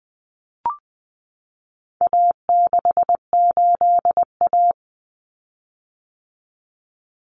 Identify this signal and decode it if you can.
Morse code, heard as a keyed tone, about 20 wpm, 700 Hz. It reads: A68A
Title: Common Cut Numbers - 4-Digit 20wpm